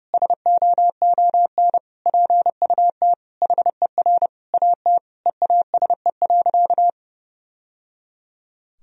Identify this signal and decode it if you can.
Morse code, heard as a keyed tone, about 30 wpm, 700 Hz. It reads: SOON PUT HER AT EASE.
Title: Sentences from Top 1000 Words - Review Part 1 - 30wpm